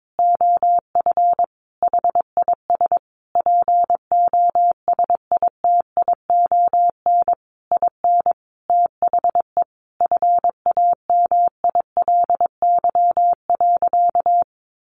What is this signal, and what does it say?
Morse code, heard as a keyed tone, about 22 wpm, 700 Hz. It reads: OF HIS POSITION IN THE FAMILY.
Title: Sentences from Top 1200 Words 22wpm